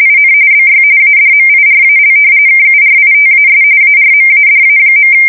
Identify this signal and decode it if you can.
RTTY, about 45.45 baud. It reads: IN BY ADMIRAL KURITAS ARMADA.
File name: fsk170 / battleshipnj